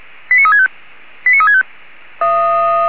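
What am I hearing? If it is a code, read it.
Five-tone selective call (ZVEI), heard as quick runs of five tones, tones of about 70 ms. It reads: 89365 89365
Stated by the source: ZVEI five-tone selective call 89360 received on SDR.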